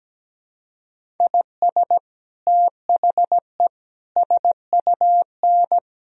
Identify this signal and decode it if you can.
Morse code, heard as a keyed tone, about 17 wpm, 700 Hz. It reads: IS THE SUN
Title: Sentences from Top 1000 Words - Review Part 1 - 17wpm